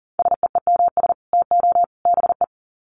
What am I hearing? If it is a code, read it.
Morse code, heard as a keyed tone, about 40 wpm, 700 Hz. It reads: SEEMS TO BE
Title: Sets of 3 Words 40wpm